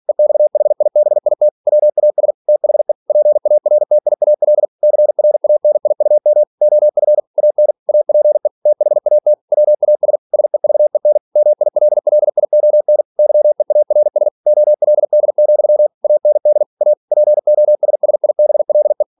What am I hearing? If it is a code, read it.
Morse code, heard as a keyed tone, about 46 wpm, 600 Hz. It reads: EXHIBIT WAS THE PARTIAL CRANIUM OF AN APE THAT WAS SEVEN MILLION YEARS OLD, AND A POSSIBLE
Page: Never 47wpm